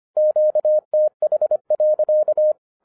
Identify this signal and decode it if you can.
Morse code, heard as a keyed tone, about 26 wpm, 610 Hz. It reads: QTH.
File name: 2011-10-mp3 / n04-JUST MOVED 25wpm